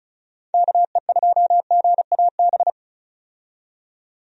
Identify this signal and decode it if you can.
Morse code, heard as a keyed tone, about 36 wpm, 700 Hz. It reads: KE2GAB
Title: USA Call Signs - 6 Characters - 35wpm